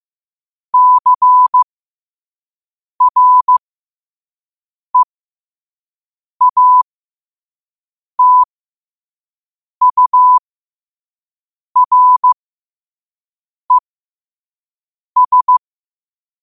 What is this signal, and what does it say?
Morse code, heard as a keyed tone, about 15 wpm, spaced as for 5 wpm, 1000 Hz. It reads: CREATURES
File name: qotdfiles / 5wpm